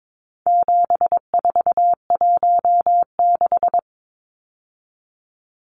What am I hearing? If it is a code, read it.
Morse code, heard as a keyed tone, about 22 wpm, 700 Hz. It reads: 7416
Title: Numbers - 4-Digit 22wpm